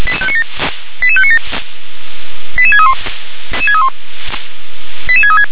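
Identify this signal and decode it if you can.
Five-tone selective call (ZVEI), heard as quick runs of five tones, tones of about 70 ms. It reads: 88597 88631 88636
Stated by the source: Received on an SDR.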